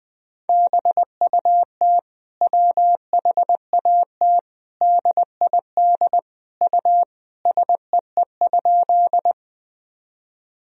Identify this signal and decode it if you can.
Morse code, heard as a keyed tone, about 20 wpm, 700 Hz. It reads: BUT WHAT DID U SEE?
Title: Sentences from Top 100 Words and Abbreviations 20wpm